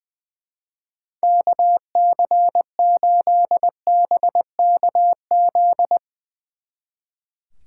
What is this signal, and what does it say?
Morse code, heard as a keyed tone, about 20 wpm, 700 Hz. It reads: KC8BKZ
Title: Speed Racing - USA Call Signs 20wpm